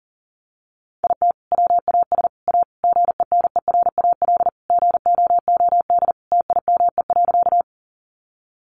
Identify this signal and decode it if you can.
Morse code, heard as a keyed tone, about 40 wpm, 700 Hz. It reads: IT WAS A GENERAL GOOD TIME.
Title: Sentences from Top 600 Words 40wpm